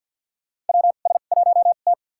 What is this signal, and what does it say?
Morse code, heard as a keyed tone, about 51 wpm, spaced as for 36 wpm, 700 Hz. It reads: WS1T
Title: USA Call Signs - 4 Characters - 2x Character Spacing - 50wpm